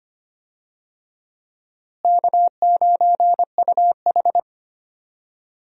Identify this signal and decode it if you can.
Morse code, heard as a keyed tone, about 25 wpm, 700 Hz. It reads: K9UH
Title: USA Call Signs - 4 Characters - Repeated 2x - 25wpm